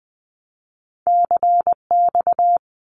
Morse code, heard as a keyed tone, about 19 wpm, 700 Hz. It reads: CX
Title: Abbreviations 20wpm